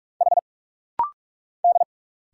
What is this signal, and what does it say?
Morse code, heard as a keyed tone, about 45 wpm, 700 Hz. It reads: H D
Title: Single Letters - Rapid-Fire 45wpm